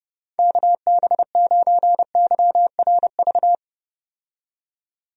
Morse code, heard as a keyed tone, about 32 wpm, 700 Hz. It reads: KB9YRV
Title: USA Call Signs - 6 Characters - 30wpm